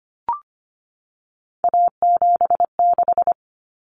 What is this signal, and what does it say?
Morse code, heard as a keyed tone, about 25 wpm, 700 Hz. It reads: A76
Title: Common Cut Numbers - 3-Digit 25wpm